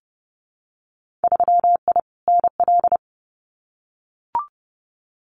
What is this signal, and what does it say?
Morse code, heard as a keyed tone, about 30 wpm, 700 Hz. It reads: 3I NL
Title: Winter Field Day Exchanges 30wpm